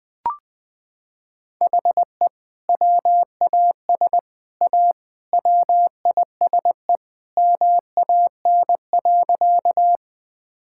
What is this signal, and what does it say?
Morse code, heard as a keyed tone, about 19 wpm, 700 Hz. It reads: HE WAS A WISE MAN.
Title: Sentences from Top 1600 Words 20wpm